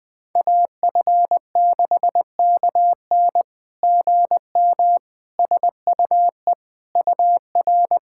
Morse code, heard as a keyed tone, about 21 wpm, 690 Hz. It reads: AF6KN GM SUE UR
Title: SOTA QSO Practice 20wpm